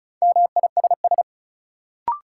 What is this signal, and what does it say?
Morse code, heard as a keyed tone, about 36 wpm, 700 Hz. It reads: MISS
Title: Sentences from Top 500 Words - New Word Drill - 35wpm